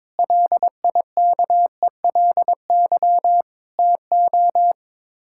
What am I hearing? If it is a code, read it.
Morse code, heard as a keyed tone, about 22 wpm, 700 Hz. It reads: LIKELY TO
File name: practice / 22wpm / Build A Phrase - 5-Word Phrases - 22wpm